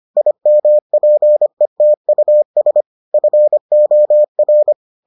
Morse code, heard as a keyed tone, about 25 wpm, 600 Hz. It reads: IMPETUS FOR